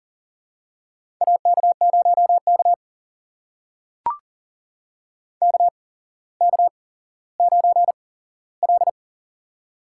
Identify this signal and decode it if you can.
Morse code, heard as a keyed tone, about 44 wpm, spaced as for 18 wpm, 700 Hz. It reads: AK0K KK9L
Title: USA Call Signs - 4 Characters - 8x Character Spacing - 40wpm